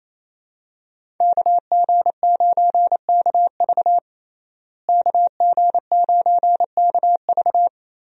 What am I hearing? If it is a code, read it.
Morse code, heard as a keyed tone, about 28 wpm, 700 Hz. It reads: KG9KV KG9KV
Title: USA Call Signs - 5 Characters - Repeated 2x - 28wpm